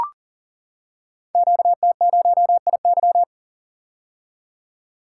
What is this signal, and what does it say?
Morse code, heard as a keyed tone, about 39 wpm, 690 Hz. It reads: QT0IY
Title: Code Groups - 5 Letter:Numbers - 40wpm